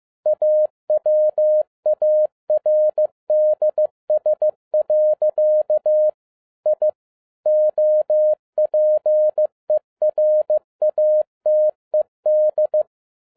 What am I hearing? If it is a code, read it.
Morse code, heard as a keyed tone, about 15 wpm, 610 Hz. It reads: AWARDS. I OPERATED
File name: 2011-10-mp3 / n12-JL3YMV 15wpm